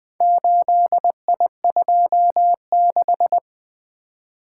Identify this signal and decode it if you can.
Morse code, heard as a keyed tone, about 21 wpm, 700 Hz. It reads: 8I26
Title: Four Letter-Number Combinations - 20wpm